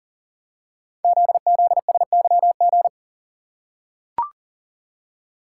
Morse code, heard as a keyed tone, about 39 wpm, 700 Hz. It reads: ZZSYG
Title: Code Groups - 5 Letters - 40wpm